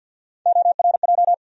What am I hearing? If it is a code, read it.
Morse code, heard as a keyed tone, about 52 wpm, 700 Hz. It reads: ORJ
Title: USA Call Sign Postfixes - 3 Letters 50wpm